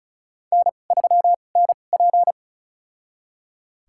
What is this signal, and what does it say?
Morse code, heard as a keyed tone, about 36 wpm, spaced as for 25 wpm, 700 Hz. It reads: N3NP
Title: USA Call Signs - 4 Characters - 2x Character Spacing - 35wpm